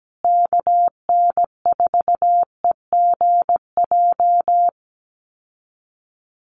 Morse code, heard as a keyed tone, about 17 wpm, 700 Hz. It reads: KN4EGJ
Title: USA Call Signs Repeated 3x 17wpm